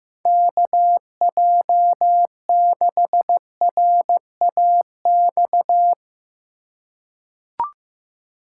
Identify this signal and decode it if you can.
Morse code, heard as a keyed tone, about 15 wpm, 700 Hz. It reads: KJ6RAX
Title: USA Call Signs - 8x Character Spacing - 15wpm